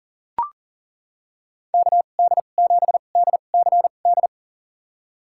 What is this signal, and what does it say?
Morse code, heard as a keyed tone, about 40 wpm, spaced as for 29 wpm, 700 Hz. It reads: KD7DCD
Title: USA Call Signs - 2x Character Spacing - 40wpm